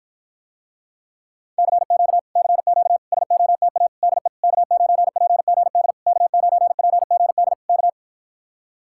Morse code, heard as a keyed tone, about 55 wpm, 700 Hz. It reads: CQ CQ SOTA DE K9PGD K9PGD K